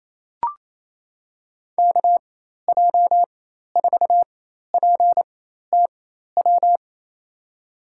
Morse code, heard as a keyed tone, about 28 wpm, spaced as for 13 wpm, 700 Hz. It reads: KJ4PTW
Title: USA Call Signs - 6 Characters - 4x Character Spacing - 28wpm